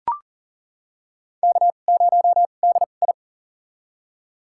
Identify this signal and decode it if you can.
Morse code, heard as a keyed tone, about 37 wpm, spaced as for 28 wpm, 700 Hz. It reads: K0DI
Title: USA Call Signs - 4 Characters - 2x Character Spacing - 40wpm